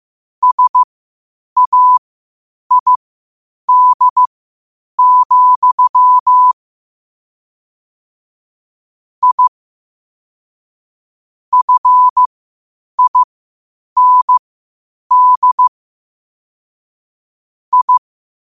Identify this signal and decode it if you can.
Morse code, heard as a keyed tone, about 15 wpm, spaced as for 8 wpm, 1000 Hz. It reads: SAID, I FIND I
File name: qotdfiles / 8wpm